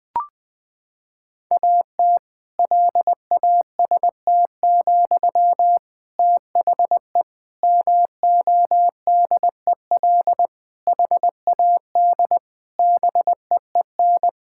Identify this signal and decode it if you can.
Morse code, heard as a keyed tone, about 20 wpm, 700 Hz. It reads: AT LAST, THE MODEL HAD BEEN
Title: Sentences from Top 1000 Words - Review Part 5 - 20wpm